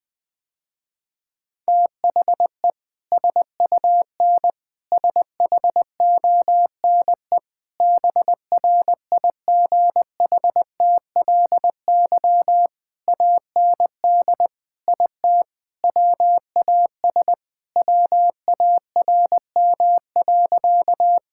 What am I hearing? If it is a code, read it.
Morse code, heard as a keyed tone, about 20 wpm, 700 Hz. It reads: THE SUN SHONE BRIGHTLY AND IT WAS WARM.
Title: Sentences with 2-Word Phrases - Spoken Beginning - 20wpm